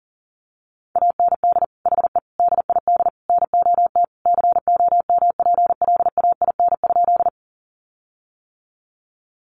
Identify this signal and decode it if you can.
Morse code, heard as a keyed tone, about 40 wpm, 700 Hz. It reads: AND HE DID NOT COMPLAIN?
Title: Sentences from Top 1300 Words 40wpm